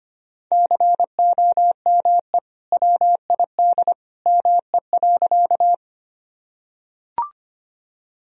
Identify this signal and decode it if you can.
Morse code, heard as a keyed tone, about 25 wpm, 700 Hz. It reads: COME WID ME.
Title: Sentences from Top 200 Words and Abbreviations 25wpm